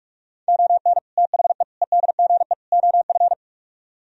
Morse code, heard as a keyed tone, about 58 wpm, spaced as for 51 wpm, 700 Hz. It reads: ON THE EDGE OF
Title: Speed-Racing - Sets of 4 Words 45wpm